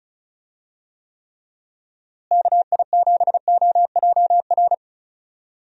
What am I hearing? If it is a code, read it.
Morse code, heard as a keyed tone, about 34 wpm, 700 Hz. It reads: KI7OJR
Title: USA Call Signs - 6 Characters - Repeated 2x - 35wpm